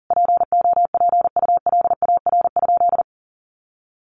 Morse code, heard as a keyed tone, about 40 wpm, 700 Hz. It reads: POPULAR?
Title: Sentences from Top 1200 Words 40wpm